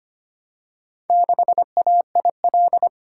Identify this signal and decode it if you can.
Morse code, heard as a keyed tone, about 25 wpm, 700 Hz. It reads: 6AIL
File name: practice / 25wpm / Four Letter-Number Combinations - 25wpm